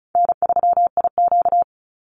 Morse code, heard as a keyed tone, about 34 wpm, 700 Hz. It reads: N3IQ
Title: Call Signs - 35wpm